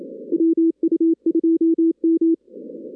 Morse code, heard as a keyed tone, about 27 wpm, 330 Hz. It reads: WU2M
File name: 2016 arrl dx cw / 2016-ARRL-DX-CW-K5TR- - 0-41